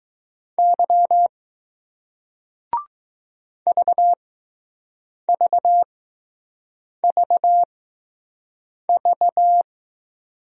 Morse code, heard as a keyed tone, about 20 wpm, spaced as for 6 wpm, 700 Hz. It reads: Y VVVV